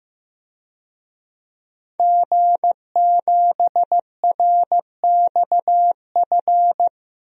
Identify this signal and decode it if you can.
Morse code, heard as a keyed tone, about 15 wpm, 700 Hz. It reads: G7RXF
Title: DXCC-100 Call Signs - 5 Characters - Repeated 2x - 15wpm